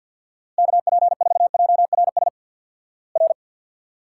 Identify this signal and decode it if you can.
Morse code, heard as a keyed tone, about 50 wpm, 700 Hz. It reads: KP4JRS R
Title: Worked All Europe DX Contest - QTC Exchange with Repeats - 50wpm